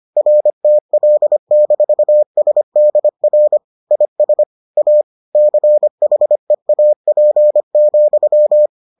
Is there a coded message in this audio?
Yes: Morse code from 0.2 to 8.7 s, heard as a keyed tone, about 25 wpm, 600 Hz. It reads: RTL-SDR IS A CHEAP,